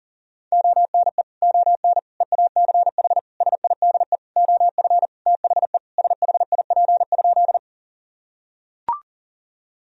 Morse code, heard as a keyed tone, about 37 wpm, 700 Hz. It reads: ONE ON EACH SIDE OF THE SHIP?
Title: Sentences from Top 1000 Words - Review Part 2 - 40wpm